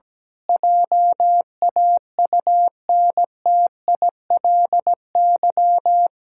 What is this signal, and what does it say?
Morse code, heard as a keyed tone, about 17 wpm, 700 Hz. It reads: JAUNTILY
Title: Pangram Challenge v2 - 17wpm